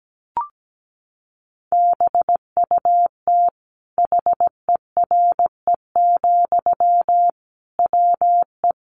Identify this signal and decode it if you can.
Morse code, heard as a keyed tone, about 17 wpm, 700 Hz. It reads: BUT HERE, WE
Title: Sentences from Top 1500 Words 17wpm